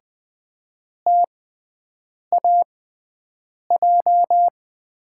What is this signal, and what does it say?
Morse code, heard as a keyed tone, about 20 wpm, spaced as for 7 wpm, 700 Hz. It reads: TAJ
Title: USA Call Signs - 5 Characters - 6x Character Spacing - 20wpm